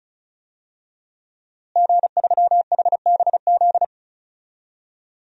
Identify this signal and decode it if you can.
Morse code, heard as a keyed tone, about 35 wpm, 700 Hz. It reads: G3HBZ
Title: DXCC-100 Call Signs - 5 Characters - Repeated 2x - 35wpm